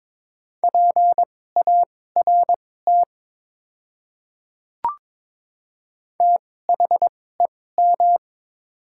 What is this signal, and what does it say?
Morse code, heard as a keyed tone, about 22 wpm, spaced as for 16 wpm, 700 Hz. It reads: PART THEM
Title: 2x Character Spacing - Top 200 Words - 22wpm